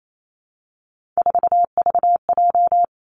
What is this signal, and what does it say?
Morse code, heard as a keyed tone, about 28 wpm, 700 Hz. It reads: 4VJ
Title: Three Challenging Letter-Number Combinations 28wpm